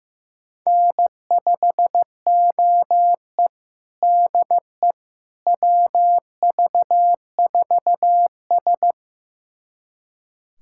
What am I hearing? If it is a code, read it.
Morse code, heard as a keyed tone, about 15 wpm, 700 Hz. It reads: N5OE DE WV4S
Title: USA Callsign DE Callsign - 4 Characters - 15wpm